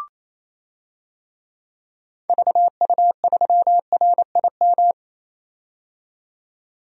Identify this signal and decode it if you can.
Morse code, heard as a keyed tone, about 28 wpm, 700 Hz. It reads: VU3RIM
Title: DXCC-100 Call Signs - 6 Characters - 28wpm